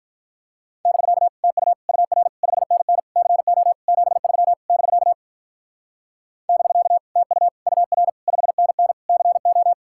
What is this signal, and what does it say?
Morse code, heard as a keyed tone, about 45 wpm, 700 Hz. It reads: <BK> TU UR 5NN CO 73 <BK> <BK> TU UR 5NN CO